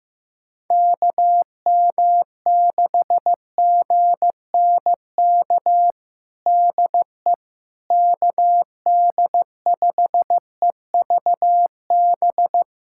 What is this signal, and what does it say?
Morse code, heard as a keyed tone, about 15 wpm, 700 Hz. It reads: KM6GNK DE KD5EVB
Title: USA Callsign DE Callsign - 6 Characters - 15wpm